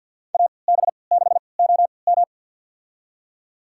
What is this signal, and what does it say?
Morse code, heard as a keyed tone, about 52 wpm, spaced as for 29 wpm, 700 Hz. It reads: AB6OM
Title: USA Call Signs - 3x Character Spacing - 50wpm